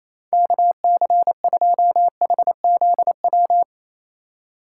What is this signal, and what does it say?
Morse code, heard as a keyed tone, about 28 wpm, 700 Hz. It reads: KC2HZW
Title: USA Call Signs - 6 Characters - 4x Character Spacing - 28wpm